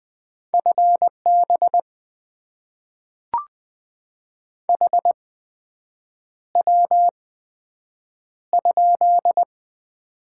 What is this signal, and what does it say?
Morse code, heard as a keyed tone, about 20 wpm, spaced as for 7 wpm, 700 Hz. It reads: FB HW?
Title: Top 20 QSO Elements - 8x Character Spacing - 20wpm